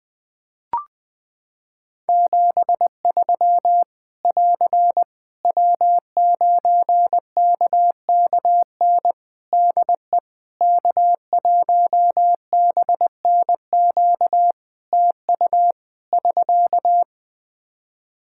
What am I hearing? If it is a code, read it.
Morse code, heard as a keyed tone, about 20 wpm, 700 Hz. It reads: 73 <AR> W9KKN DE K1BNQ TU <SK>
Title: K1USN SST QSO Practice 20wpm